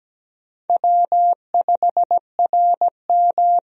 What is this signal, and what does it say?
Morse code, heard as a keyed tone, about 17 wpm, 700 Hz. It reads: W5RM
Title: USA Call Signs - 4 Characters - 17wpm